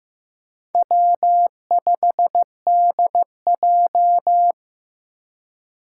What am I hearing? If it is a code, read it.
Morse code, heard as a keyed tone, about 15 wpm, 700 Hz. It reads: W5DJ
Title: Four Challenging Letter-Number Combinations 15wpm